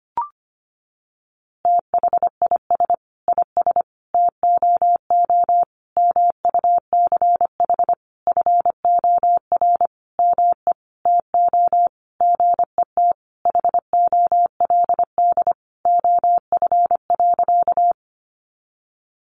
Morse code, heard as a keyed tone, about 25 wpm, 700 Hz. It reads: THIS IS TOO MUCH FOR ME TO GET HOLD OF.
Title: A Sound Of Thunder 25wpm